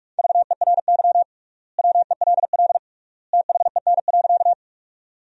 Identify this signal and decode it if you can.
Morse code, heard as a keyed tone, about 46 wpm, 700 Hz. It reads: VERY WELL THEN.
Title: 3x Word-Spacing - Sentences from Top 200 Words - 45wpm